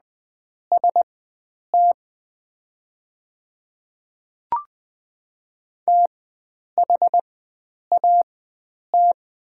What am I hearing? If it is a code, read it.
Morse code, heard as a keyed tone, about 20 wpm, spaced as for 9 wpm, 690 Hz. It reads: ST THAT
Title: 4x Character Spacing - Top 100 Words - 20wpm